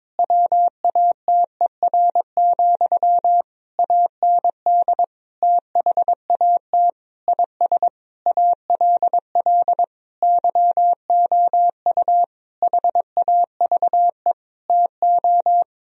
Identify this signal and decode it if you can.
Morse code, heard as a keyed tone, about 22 wpm, 700 Hz. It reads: WATER, AND THAT IS ALL YOU HAVE TO
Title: Sentences from Top 1000 Words - Review Part 2 - 22wpm